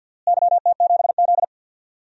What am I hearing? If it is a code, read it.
Morse code, heard as a keyed tone, about 52 wpm, 700 Hz. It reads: YT7Z
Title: DXCC-100 Call Signs - 4 Characters - Repeated 2x - 50wpm